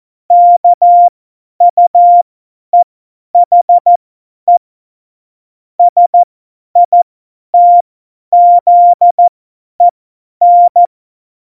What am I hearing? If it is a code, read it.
Morse code, heard as a keyed tone, about 14 wpm, spaced as for 10 wpm, 700 Hz. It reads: KUEHE SITZEN
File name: Morsekurs / Witze / Witz 29. Mai 2018, 50 F70